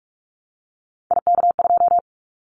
Morse code, heard as a keyed tone, about 44 wpm, 700 Hz. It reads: IK2
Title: Callsign Prefixes for Top 50 DXCC - 45wpm